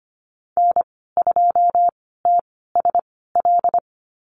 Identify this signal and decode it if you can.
Morse code, heard as a keyed tone, about 25 wpm, spaced as for 16 wpm, 700 Hz. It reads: N2TSL
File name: Lesson 15g - Number 5 - Call Signs 25wpm